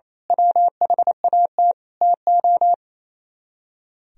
Morse code, heard as a keyed tone, about 29 wpm, 700 Hz. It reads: WHAT TO
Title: Speed-Racing - Sets of 5 Words 28wpm